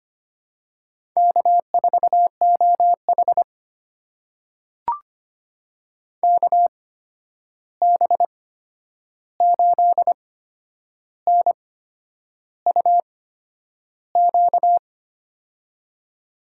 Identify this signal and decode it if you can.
Morse code, heard as a keyed tone, about 25 wpm, spaced as for 9 wpm, 700 Hz. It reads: K4OH KB8NUQ
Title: USA Call Signs - 8x Character Spacing - 25wpm